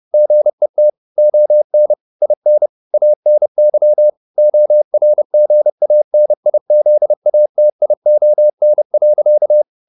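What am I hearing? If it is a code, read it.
Morse code, heard as a keyed tone, about 30 wpm, 600 Hz. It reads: GET ON IN ANY ORGANIZATION.
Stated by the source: Morse code at 30 wpm, the novel Never by Ken Follett in English, from CW med Gnister.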